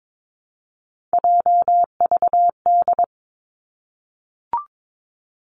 Morse code, heard as a keyed tone, about 22 wpm, 700 Hz. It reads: JVD
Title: Three Challenging Letter-Number Combinations 22wpm